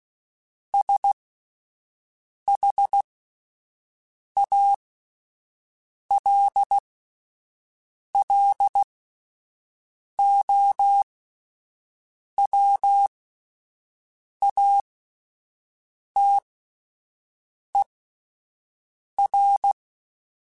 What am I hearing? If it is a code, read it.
Morse code, heard as a keyed tone, about 16 wpm, spaced as for 5 wpm, 800 Hz. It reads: SHALLOWATER